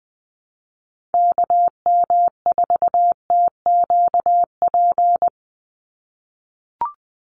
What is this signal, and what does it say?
Morse code, heard as a keyed tone, about 20 wpm, 700 Hz. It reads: KM4TQP
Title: USA Call Signs 20wpm